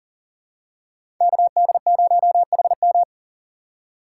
Morse code, heard as a keyed tone, about 40 wpm, 700 Hz. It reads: KD0HM
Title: USA Call Signs - 5 Characters - 6x Character Spacing - 40wpm